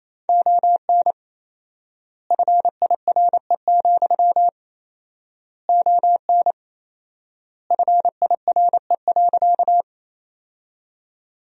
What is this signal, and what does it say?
Morse code, heard as a keyed tone, about 28 wpm, 700 Hz. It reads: ON FIRE, ON FIRE.
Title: 4x Word-Spacing - Sentences from Top 400 Words - 28wpm